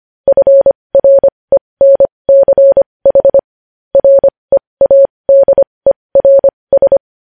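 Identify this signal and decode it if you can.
Morse code, heard as a keyed tone, about 25 wpm, spaced as for 20 wpm, 550 Hz. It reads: FRENCH READERS
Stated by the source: Morse code practice sending news headlines at 20 wpm.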